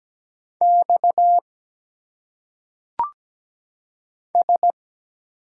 Morse code, heard as a keyed tone, about 17 wpm, 700 Hz. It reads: X S
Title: Single Letters - Phonetic Pronunciation 17wpm